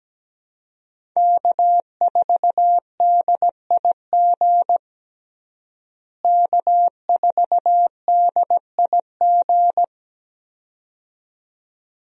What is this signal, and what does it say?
Morse code, heard as a keyed tone, about 17 wpm, 700 Hz. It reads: K4DIG K4DIG
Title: USA Call Signs - 5 Characters - Repeated 2x - 17wpm